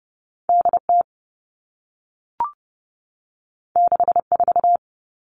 Morse code, heard as a keyed tone, about 30 wpm, 700 Hz. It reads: D T 6 4
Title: All Cut Numbers - 2-Digit 30wpm